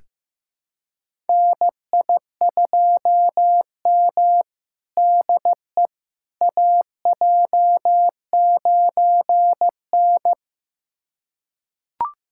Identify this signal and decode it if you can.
Morse code, heard as a keyed tone, about 15 wpm, 700 Hz. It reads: NI2M DE AJ9N
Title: USA Callsign DE Callsign - 4 Characters - 15wpm